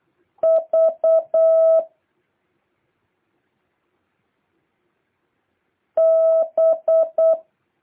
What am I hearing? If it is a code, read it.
Morse code, heard as a keyed tone, about 8 wpm, 650 Hz. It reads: V B
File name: jh8cbh / 2023cw